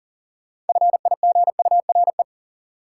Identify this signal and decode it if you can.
Morse code, heard as a keyed tone, about 40 wpm, 700 Hz. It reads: FIGURE
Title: Sentences from Top 1400 Words - New Word Drill Alt - 40wpm